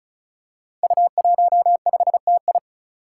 Morse code, heard as a keyed tone, about 36 wpm, 700 Hz. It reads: U15TI
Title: Five Challenging Letter-Number Combinations 35wpm